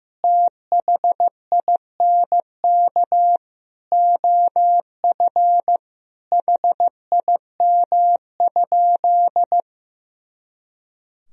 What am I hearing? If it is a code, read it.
Morse code, heard as a keyed tone, about 15 wpm, 700 Hz. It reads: THINK OF HIM?
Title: Sentences from Top 200 Words and Abbreviations 15wpm